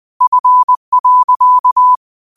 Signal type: Morse code